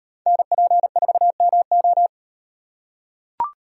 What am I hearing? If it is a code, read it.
Morse code, heard as a keyed tone, about 38 wpm, 700 Hz. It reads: NP4MO